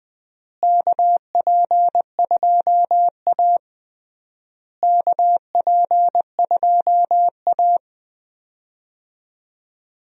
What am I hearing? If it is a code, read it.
Morse code, heard as a keyed tone, about 20 wpm, 700 Hz. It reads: KP2A KP2A
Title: DXCC-100 Call Signs - 4 Characters - Repeated 3x - 20wpm